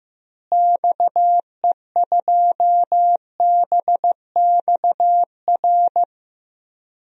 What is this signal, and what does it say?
Morse code, heard as a keyed tone, about 15 wpm, 700 Hz. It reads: XE2BXR
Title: DXCC-100 Call Signs - 6 Characters - Repeated 3x - 15wpm